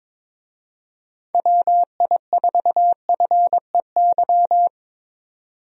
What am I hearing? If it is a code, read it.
Morse code, heard as a keyed tone, about 22 wpm, 700 Hz. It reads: WI4FEY
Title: USA Call Signs - 6 Characters - 22wpm